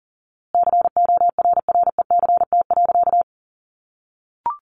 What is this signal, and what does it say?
Morse code, heard as a keyed tone, about 37 wpm, 700 Hz. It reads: CORRECT.
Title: A Sound Of Thunder 40wpm